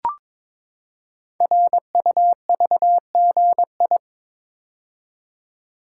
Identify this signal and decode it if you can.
Morse code, heard as a keyed tone, about 22 wpm, 700 Hz. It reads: RUVGI
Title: Code Groups - 5 Letter:Numbers - 22wpm